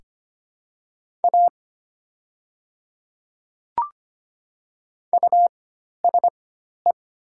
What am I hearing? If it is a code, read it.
Morse code, heard as a keyed tone, about 25 wpm, spaced as for 12 wpm, 700 Hz. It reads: A USE
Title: 4x Character Spacing - Top 200 Words - 25wpm